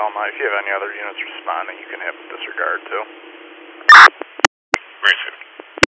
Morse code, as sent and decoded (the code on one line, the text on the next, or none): none